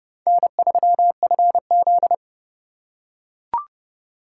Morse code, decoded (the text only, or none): N3FZ